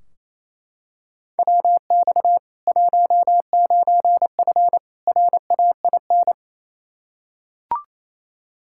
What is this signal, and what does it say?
Morse code, heard as a keyed tone, about 28 wpm, 700 Hz. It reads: WX 19F RAIN
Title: K1USN SST QSO Practice 28wpm